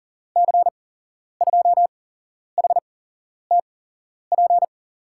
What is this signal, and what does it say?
Morse code, heard as a keyed tone, about 40 wpm, spaced as for 11 wpm, 700 Hz. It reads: C2HTP
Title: USA Call Signs - 6 Characters - 8x Character Spacing - 40wpm